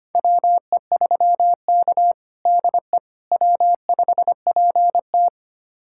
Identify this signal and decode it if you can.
Morse code, heard as a keyed tone, about 25 wpm, 700 Hz. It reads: WE3K DE W5PT
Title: USA Callsign DE Callsign - 4 Characters - 25wpm